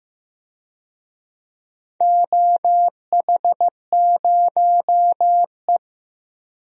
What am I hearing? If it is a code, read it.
Morse code, heard as a keyed tone, about 15 wpm, 700 Hz. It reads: OH0E
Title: DXCC-100 Call Signs - 4 Characters - 15wpm